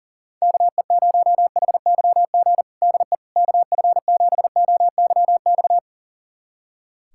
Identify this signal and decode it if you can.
Morse code, heard as a keyed tone, about 40 wpm, 700 Hz. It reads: KE0HYG DE KF7OYX
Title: USA Callsign DE Callsign - 6 Characters - 40wpm